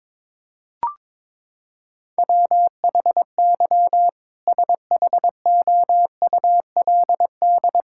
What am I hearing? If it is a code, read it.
Morse code, heard as a keyed tone, about 22 wpm, 700 Hz. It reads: WHY SHOULD
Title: Sentences from Top 1600 Words 22wpm